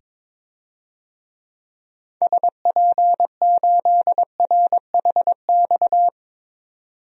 Morse code, heard as a keyed tone, about 22 wpm, 700 Hz. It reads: SP8RHX
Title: DXCC-100 Call Signs - 6 Characters - 22wpm